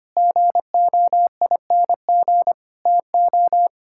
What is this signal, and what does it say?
Morse code, heard as a keyed tone, about 25 wpm, 700 Hz. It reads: GOING TO
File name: practice / 25wpm / Sentences with 4-Word Phrases - 25wpm